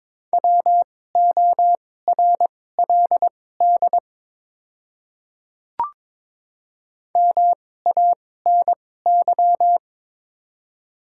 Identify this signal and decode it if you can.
Morse code, heard as a keyed tone, about 22 wpm, spaced as for 16 wpm, 700 Hz. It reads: WORLD MANY
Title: 2x Character Spacing - Top 200 Words - 22wpm